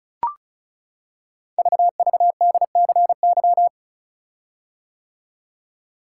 Morse code, heard as a keyed tone, about 36 wpm, 700 Hz. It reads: VVDCY
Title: Code Groups - 5 Letters - 35wpm